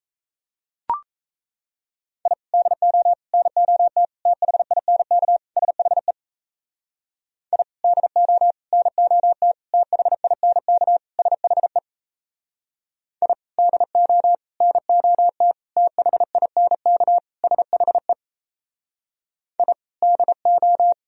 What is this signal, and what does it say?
Morse code, heard as a keyed tone, about 36 wpm, 700 Hz. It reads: I DO NOT THINK SHE I DO NOT THINK SHE I DO NOT THINK SHE I DO